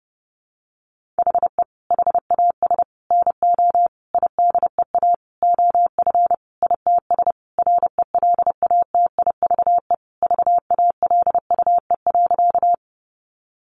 Morse code, heard as a keyed tone, about 30 wpm, 700 Hz. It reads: HE HAS NO IDEA OF ITS RELATIVE VALUE.
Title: Sentences from Top 1400 Words 30wpm